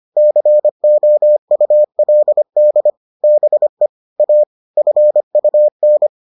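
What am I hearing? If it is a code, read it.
Morse code, heard as a keyed tone, about 25 wpm, 600 Hz. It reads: COULD BE A FUN
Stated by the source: Morse edition of the Foundations of Amateur Radio episodes.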